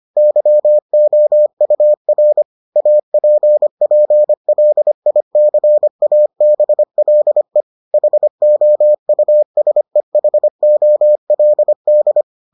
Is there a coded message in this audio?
Yes: Morse code from 0.2 to 12.2 s, heard as a keyed tone, about 25 wpm, 600 Hz. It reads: YOUR APPLICABLE HOUSEHOLD